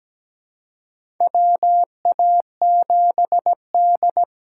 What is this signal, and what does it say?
Morse code, heard as a keyed tone, about 17 wpm, 700 Hz. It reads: WA7D
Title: USA Call Signs - 4 Characters - 2x Character Spacing - 17wpm